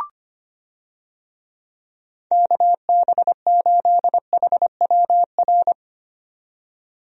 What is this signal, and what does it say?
Morse code, heard as a keyed tone, about 25 wpm, 700 Hz. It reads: KB8HWR
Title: USA Call Signs - 6 Characters - 25wpm